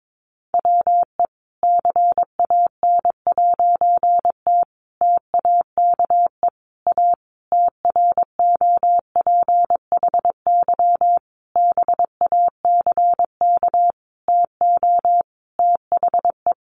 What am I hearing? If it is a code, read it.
Morse code, heard as a keyed tone, about 22 wpm, 700 Hz. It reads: WE CAN'T TAKE A TROPHY BACK TO THE
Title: A Sound Of Thunder 22wpm